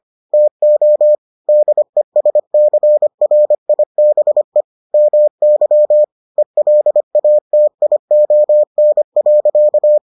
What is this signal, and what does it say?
Morse code, heard as a keyed tone, about 25 wpm, 600 Hz. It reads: TO DESCRIBE MY ELATION.